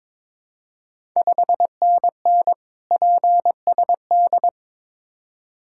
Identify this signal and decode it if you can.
Morse code, heard as a keyed tone, about 22 wpm, 690 Hz. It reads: 5NN PSD
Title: Ontario QSO Party Exchange 22wpm